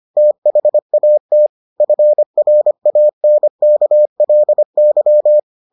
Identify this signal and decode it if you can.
Morse code, heard as a keyed tone, about 25 wpm, 600 Hz. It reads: THAT FRANKLY